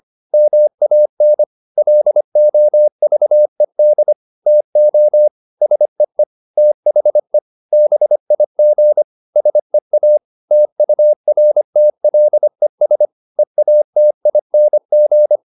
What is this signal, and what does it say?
Morse code, heard as a keyed tone, about 25 wpm, 600 Hz. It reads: MAN LOVED TO SEE THE BIG SEA TURTLES EATING
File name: the old man and the sea part 1 25wpm